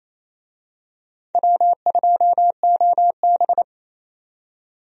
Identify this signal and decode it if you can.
Morse code, heard as a keyed tone, about 28 wpm, 700 Hz. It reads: W2OB
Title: USA Call Signs - 4 Characters - 28wpm